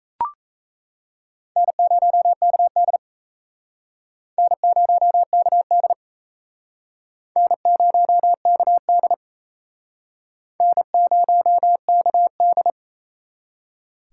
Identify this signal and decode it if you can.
Morse code, heard as a keyed tone, about 38 wpm, 700 Hz. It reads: N0KD N0KD N0KD N0KD